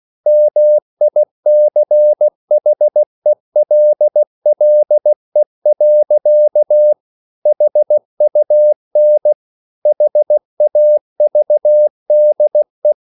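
Morse code, heard as a keyed tone, about 16 wpm, 600 Hz. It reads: MICHELLE. HUN HAVDE
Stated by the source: Slow (QRS) Morse code at 16 wpm, from CW med Gnister.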